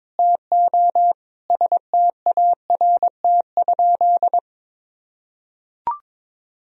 Morse code, heard as a keyed tone, about 22 wpm, 700 Hz. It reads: TO START?
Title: Sentences from Top 1000 Words - Review Part 1 - 22wpm